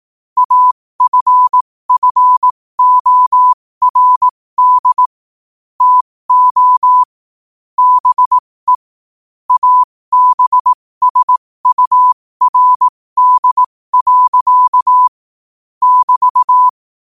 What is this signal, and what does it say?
Morse code, heard as a keyed tone, about 18 wpm, spaced as for 15 wpm, 1000 Hz. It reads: AFFORD TO BE ABSURD. <BT>
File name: qotdfiles / 15wpm